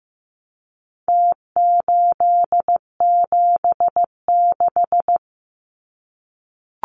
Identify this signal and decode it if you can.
Morse code, heard as a keyed tone, about 15 wpm, 700 Hz. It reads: T876
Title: Numbers - Contest Style - 4-Digit 15wpm